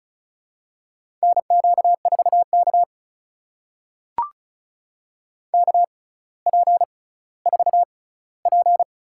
Morse code, heard as a keyed tone, about 36 wpm, spaced as for 18 wpm, 700 Hz. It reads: NQ4K KP4P
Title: USA Call Signs - 4 Characters - 6x Character Spacing - 35wpm